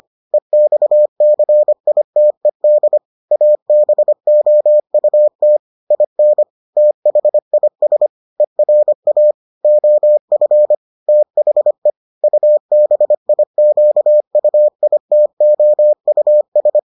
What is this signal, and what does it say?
Morse code, heard as a keyed tone, about 25 wpm, 600 Hz. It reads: EXCITED ABOUT IN THIS ERA OF THE UBIQUITOUS